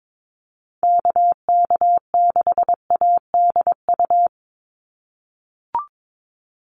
Morse code, heard as a keyed tone, about 22 wpm, 700 Hz. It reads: KK6ADU
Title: USA Call Signs 22wpm